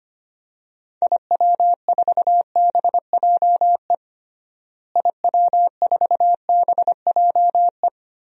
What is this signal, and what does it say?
Morse code, heard as a keyed tone, about 25 wpm, 700 Hz. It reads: IW4BJE IW4BJE
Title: DXCC-100 Call Signs - 6 Characters - Repeated 2x - 25wpm